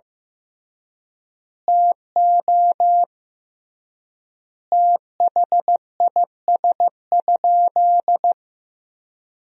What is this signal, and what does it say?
Morse code, heard as a keyed tone, about 15 wpm, 700 Hz. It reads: TO THIS?
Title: 3x Word-Spacing - Sentences from Top 100 Words - 15wpm